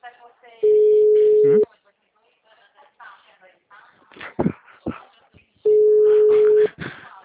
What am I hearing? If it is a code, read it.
Ringback tone (Europe, 425 Hz), with speech.
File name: out- -2006- - - .866